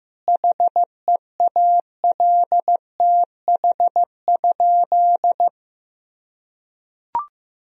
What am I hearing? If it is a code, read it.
Morse code, heard as a keyed tone, about 15 wpm, 700 Hz. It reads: HEALTH?
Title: Sentences from Top 1000 Words - Review Part 3 - 15wpm